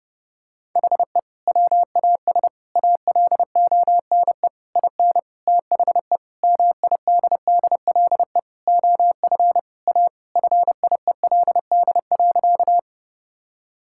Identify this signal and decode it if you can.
Morse code, heard as a keyed tone, about 30 wpm, 700 Hz. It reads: HE WAS ALONE IN THE MIDDLE OF A FIELD.
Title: Sentences from Top 1000 Words - Review Part 6 - 30wpm